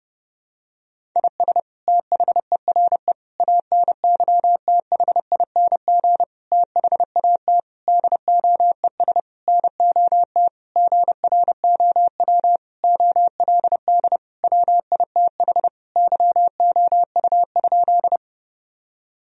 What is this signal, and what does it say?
Morse code, heard as a keyed tone, about 30 wpm, 700 Hz. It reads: IS THERE ANYTHING THAT DOES NOT GROW OLD WITH YOU?
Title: Sentences from Top 1000 Words - Review Part 1 - 30wpm